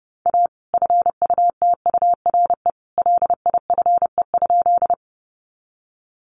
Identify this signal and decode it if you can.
Morse code, heard as a keyed tone, about 30 wpm, 700 Hz. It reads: A FUTURE LIFE?
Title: Sentences from Top 700 Words 30wpm